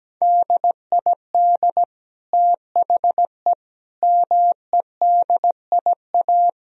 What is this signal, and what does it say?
Morse code, heard as a keyed tone, about 17 wpm, 700 Hz. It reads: DID THE MEDIA
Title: Sentences from Top 1000 Words - Review Part 4 - 17wpm